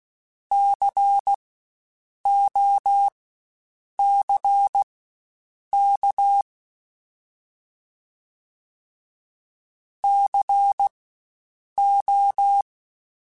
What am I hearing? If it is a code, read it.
Morse code, heard as a keyed tone, about 16 wpm, spaced as for 7 wpm, 800 Hz. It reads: COCK CO
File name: Spaced Alphas / Spaced Alphas mp3s / Cs sp @15